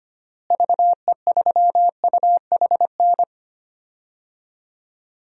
Morse code, heard as a keyed tone, about 25 wpm, 700 Hz. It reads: VE3UHN
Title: DXCC-100 Call Signs - 6 Characters - Repeated 2x - 25wpm